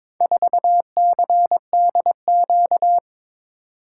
Morse code, heard as a keyed tone, about 22 wpm, 700 Hz. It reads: 4CDQ
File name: practice / 22wpm / Four Letter-Number Combinations - 22wpm